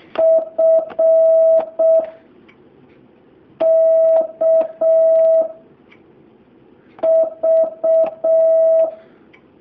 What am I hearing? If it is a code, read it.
Morse code, heard as a keyed tone, about 6 wpm, 650 Hz. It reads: F K V
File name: jh8cbh / 2019cw